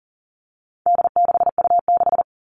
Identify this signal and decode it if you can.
Morse code, heard as a keyed tone, about 40 wpm, 700 Hz. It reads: D6U6
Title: All Cut Numbers - 4-Digit 40wpm